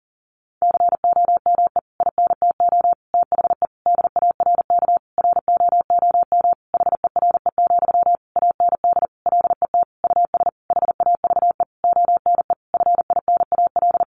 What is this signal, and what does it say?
Morse code, heard as a keyed tone, about 40 wpm, 700 Hz. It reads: COME INTO THE DARK ROOM HERE, AND LET US HAVE ONE FINAL
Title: Sentences from Top 1000 Words 40wpm